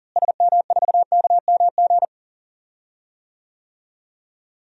Morse code, heard as a keyed tone, about 38 wpm, 700 Hz. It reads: SM4KMG
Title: DXCC-100 Call Signs - 6 Characters - Repeated 3x - 40wpm